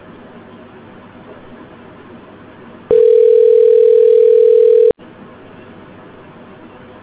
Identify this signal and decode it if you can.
Ringback tone (North America), 440 + 480 Hz.